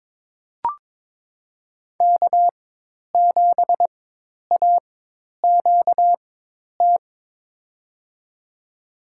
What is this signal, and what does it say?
Morse code, heard as a keyed tone, about 22 wpm, spaced as for 10 wpm, 700 Hz. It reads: K7AQT
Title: USA Call Signs - 5 Characters - 4x Character Spacing - 22wpm